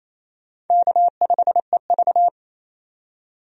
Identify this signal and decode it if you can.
Morse code, heard as a keyed tone, about 28 wpm, 700 Hz. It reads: K5EV
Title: USA Call Signs - 4 Characters - 28wpm